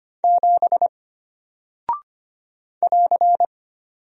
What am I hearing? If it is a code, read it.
Morse code, heard as a keyed tone, about 25 wpm, 700 Hz. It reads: 7 <AR>
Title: Single Character (Letter:Number:Punctuation:Prosign) Alt 25wpm